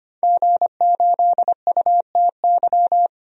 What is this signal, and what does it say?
Morse code, heard as a keyed tone, about 25 wpm, 700 Hz. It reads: G8UTY
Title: DXCC-100 Call Signs - 5 Characters - 25wpm